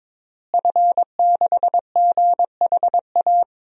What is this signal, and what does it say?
Morse code, heard as a keyed tone, about 21 wpm, 700 Hz. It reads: F6GHA
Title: DXCC-100 Call Signs - 5 Characters - Repeated 2x - 22wpm